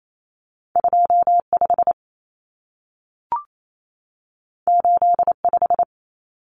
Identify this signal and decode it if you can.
Morse code, heard as a keyed tone, about 28 wpm, 700 Hz. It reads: 25 85
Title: Numbers - 2-Digit 28wpm